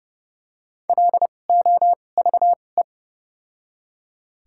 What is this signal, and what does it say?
Morse code, heard as a keyed tone, about 32 wpm, spaced as for 22 wpm, 700 Hz. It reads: LOVE
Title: 2x Character Spacing - Top 500 Words - 30wpm